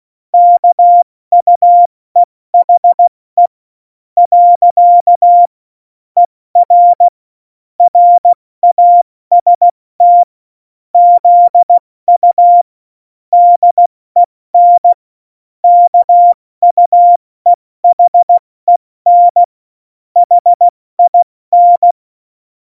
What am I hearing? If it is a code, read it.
Morse code, heard as a keyed tone, about 16 wpm, 700 Hz. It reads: KUEHE . ER RAST ZU DEN KUEHEN HIN